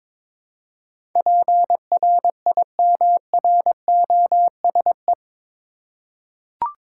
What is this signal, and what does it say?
Morse code, heard as a keyed tone, about 22 wpm, 700 Hz. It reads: PRIMROSE
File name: Difficult Words with Hints - 22wpm